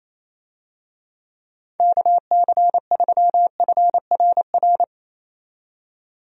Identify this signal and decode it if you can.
Morse code, heard as a keyed tone, about 27 wpm, 700 Hz. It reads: KC3FRR
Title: USA Call Signs - 6 Characters - 28wpm